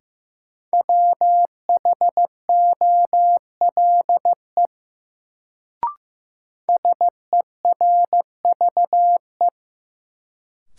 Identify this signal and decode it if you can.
Morse code, heard as a keyed tone, about 15 wpm, 700 Hz. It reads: WHOLE SERVE
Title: Sentences from Top 400 Words - New Word Drill - 15wpm